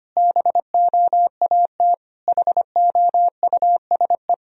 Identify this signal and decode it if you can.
Morse code, heard as a keyed tone, about 25 wpm, 700 Hz. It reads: BOAT HOUSE
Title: Sentences from Top 1000 Words - Review Part 3 - 25wpm